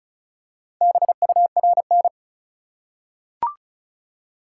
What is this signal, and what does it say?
Morse code, heard as a keyed tone, about 36 wpm, 700 Hz. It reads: BURN
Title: Four Letter Words - 35wpm